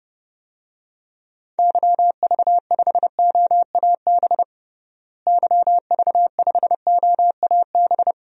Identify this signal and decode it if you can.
Morse code, heard as a keyed tone, about 32 wpm, 700 Hz. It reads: YV5OAB YV5OAB
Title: DXCC-100 Call Signs - 6 Characters - Repeated 3x - 30wpm